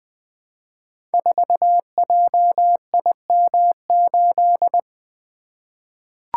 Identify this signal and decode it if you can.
Morse code, heard as a keyed tone, about 20 wpm, 700 Hz. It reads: 4JIM8
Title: Code Groups - 5 Letter:Numbers - 20wpm